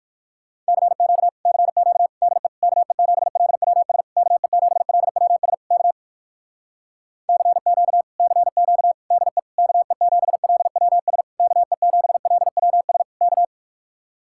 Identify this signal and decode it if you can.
Morse code, heard as a keyed tone, about 58 wpm, 700 Hz. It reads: CQ CQ DE KE7LWS KE7LWS K CQ CQ DE KE7LWS KE7LWS K